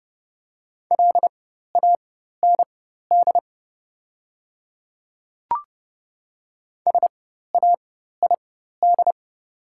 Morse code, heard as a keyed tone, about 30 wpm, spaced as for 14 wpm, 700 Hz. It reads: LAND SAID
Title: 4x Character Spacing - Top 300 Words - 30wpm